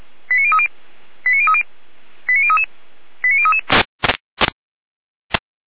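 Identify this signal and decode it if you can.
Five-tone selective call (ZVEI), heard as quick runs of five tones, tones of about 70 ms. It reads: 89030 89030 89033 89033
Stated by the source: ZVEI five-tone call received by SDR, from the audio server of feuerwehr-orlamuende.org (fire brigade Orlamünde).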